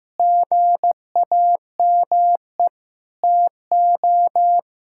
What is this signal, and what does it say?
Morse code, heard as a keyed tone, about 16 wpm, 690 Hz. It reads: GAME TO
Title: Sentences from Top 1000 Words - Review Part 5 - 15wpm